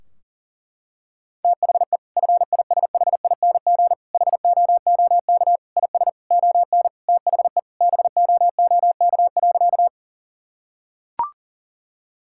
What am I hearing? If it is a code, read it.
Morse code, heard as a keyed tone, about 37 wpm, 700 Hz. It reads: THE FISHING HOOK IS ON THE BOOK.
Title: Sentences from Top 1600 Words 40wpm